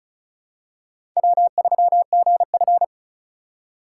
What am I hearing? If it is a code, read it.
Morse code, heard as a keyed tone, about 36 wpm, 700 Hz. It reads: W3GF
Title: USA Call Signs - 4 Characters - 3x Character Spacing - 35wpm